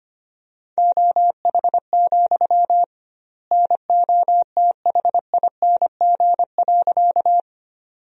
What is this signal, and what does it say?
Morse code, heard as a keyed tone, about 25 wpm, 700 Hz. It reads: OH, NOTHING.
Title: 2x Word-Spacing - Sentences from Top 500 Words - 25wpm